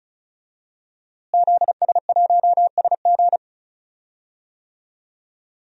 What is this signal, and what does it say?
Morse code, heard as a keyed tone, about 36 wpm, 700 Hz. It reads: ZS1SG
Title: DXCC-100 Call Signs - 5 Characters - Repeated 2x - 35wpm